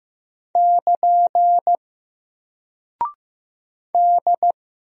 Morse code, heard as a keyed tone, about 15 wpm, 700 Hz. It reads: <KN> D
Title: Single Character (Letter:Number:Punctuation:Prosign) 15wpm